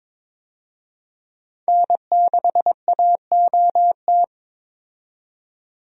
Morse code, heard as a keyed tone, about 22 wpm, 700 Hz. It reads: N6AOT
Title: USA Call Signs - 5 Characters - 22wpm